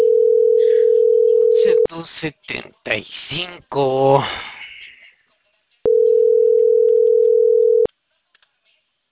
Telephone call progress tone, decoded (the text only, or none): ringback tone